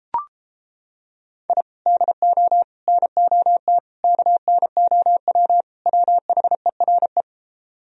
Morse code, heard as a keyed tone, about 29 wpm, 700 Hz. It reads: I DO NOT KNOW WHERE